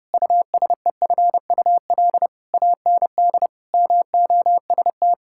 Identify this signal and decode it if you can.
Morse code, heard as a keyed tone, about 28 wpm, 700 Hz. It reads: USEFUL AND MOST
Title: Sentences from Top 1000 Words - Review Part 6 - 30wpm